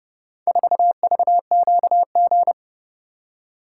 Morse code, heard as a keyed tone, about 29 wpm, 700 Hz. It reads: 4VQG